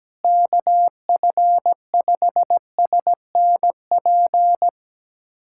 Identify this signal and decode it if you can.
Morse code, heard as a keyed tone, about 17 wpm, 700 Hz. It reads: KF5SNP
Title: USA Callsign DE Callsign - 6 Characters - 17wpm